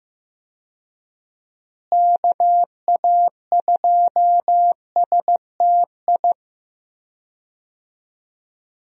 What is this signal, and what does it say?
Morse code, heard as a keyed tone, about 15 wpm, 700 Hz. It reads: KA2STI
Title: USA Call Signs - 6 Characters - Repeated 2x - 15wpm